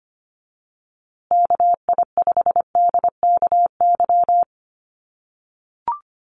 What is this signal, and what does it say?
Morse code, heard as a keyed tone, about 25 wpm, 700 Hz. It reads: KI5DKY
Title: USA Call Signs Repeated 2x 25wpm